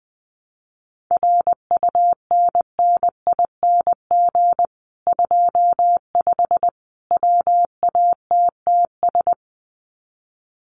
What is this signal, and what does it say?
Morse code, heard as a keyed tone, about 20 wpm, 700 Hz. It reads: RUNNING 25 WATTS
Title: Common QSO Phrases 20wpm